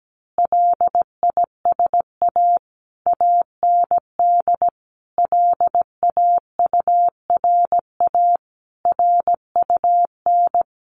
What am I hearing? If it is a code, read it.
Morse code, heard as a keyed tone, about 17 wpm, 700 Hz. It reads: LISA AND LAURA RUN
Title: Additional Easy Phrases 17wpm